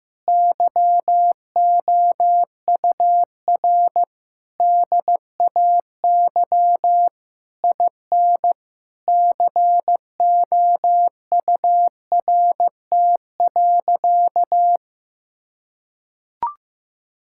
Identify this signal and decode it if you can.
Morse code, heard as a keyed tone, about 15 wpm, 700 Hz. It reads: YOUR DAY IN COURT.
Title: Sentences from Top 1600 Words 15wpm